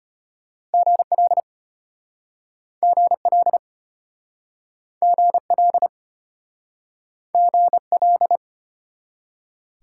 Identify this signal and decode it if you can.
Morse code, heard as a keyed tone, about 33 wpm, 700 Hz. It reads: GL GL GL GL